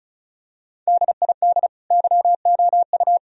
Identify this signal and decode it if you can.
Morse code, heard as a keyed tone, about 36 wpm, 700 Hz. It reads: DID YOU
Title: Sentences from Top 1000 Words - Review Part 4 - 35wpm